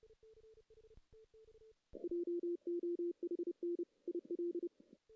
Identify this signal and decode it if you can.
Morse code, heard as a keyed tone, about 30 wpm, 350 Hz. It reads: JOHN IL
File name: 2016 JAN CW NAQP / 2016 Jan naqp k5ot - 0-2